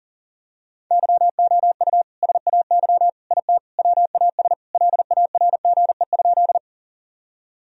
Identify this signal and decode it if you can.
Morse code, heard as a keyed tone, about 40 wpm, 700 Hz. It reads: YOU SAY IT WAS LARGE?
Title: Sentences from Top 1000 Words - Review Part 1 - 40wpm